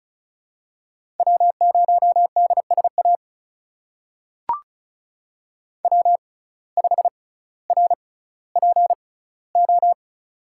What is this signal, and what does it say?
Morse code, heard as a keyed tone, about 36 wpm, spaced as for 18 wpm, 700 Hz. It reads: W0DSA W5RPO
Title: USA Call Signs - 5 Characters - 6x Character Spacing - 35wpm